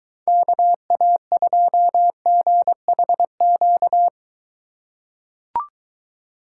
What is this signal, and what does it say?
Morse code, heard as a keyed tone, about 22 wpm, 700 Hz. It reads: KA2GHQ